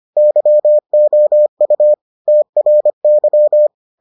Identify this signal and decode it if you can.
Morse code, heard as a keyed tone, about 25 wpm, 600 Hz. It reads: YOU TRY